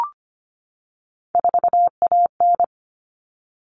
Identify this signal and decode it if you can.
Morse code, heard as a keyed tone, about 25 wpm, 700 Hz. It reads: 4AN
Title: All Cut Numbers - 3-Digit 25wpm